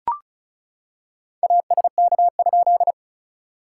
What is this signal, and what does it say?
Morse code, heard as a keyed tone, about 35 wpm, 690 Hz. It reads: ASK?
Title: SOTA QSO Practice 35wpm